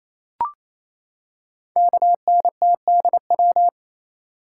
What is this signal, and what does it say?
Morse code, heard as a keyed tone, about 28 wpm, 700 Hz. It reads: KNTDW
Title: Code Groups - 5 Letters - 28wpm